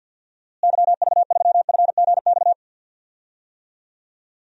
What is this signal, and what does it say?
Morse code, heard as a keyed tone, about 52 wpm, 700 Hz. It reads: YV3FGX
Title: DXCC-100 Call Signs - 6 Characters - Repeated 2x - 50wpm